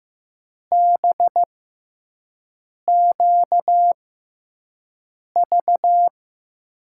Morse code, heard as a keyed tone, about 15 wpm, spaced as for 5 wpm, 700 Hz. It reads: BQV
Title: USA Call Signs - 6x Character Spacing - 15wpm